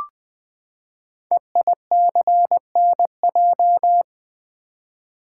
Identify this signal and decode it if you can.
Morse code, heard as a keyed tone, about 20 wpm, 700 Hz. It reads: EICNJ
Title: Code Groups - 5 Letters - 20wpm